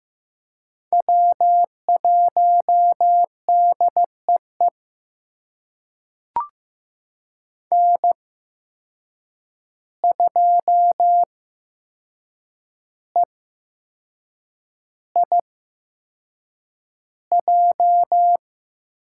Morse code, heard as a keyed tone, about 15 wpm, spaced as for 6 wpm, 700 Hz. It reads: W1DEE N2EIJ
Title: USA Call Signs - 5 Characters - 8x Character Spacing - 15wpm